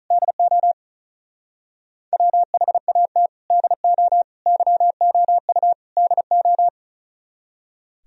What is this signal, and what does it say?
Morse code, heard as a keyed tone, about 42 wpm, 700 Hz. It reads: DO WHAT DO YOU DO
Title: Speed-Racing - Sets of 4 Words 35wpm